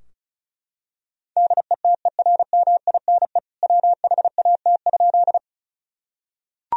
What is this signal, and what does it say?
Morse code, heard as a keyed tone, about 34 wpm, 700 Hz. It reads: DETERMINE WHAT?
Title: Sentences from Top 1000 Words - Review Part 4 - 35wpm